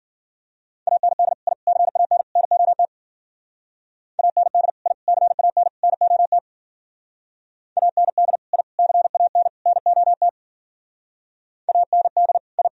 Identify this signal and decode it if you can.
Morse code, heard as a keyed tone, about 51 wpm, 700 Hz. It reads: AND I CAN NOT AND I CAN NOT AND I CAN NOT AND I